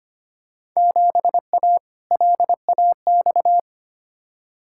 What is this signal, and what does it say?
Morse code, heard as a keyed tone, about 25 wpm, 700 Hz. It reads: 7A LAX
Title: Field Day Exchanges Encore - 25wpm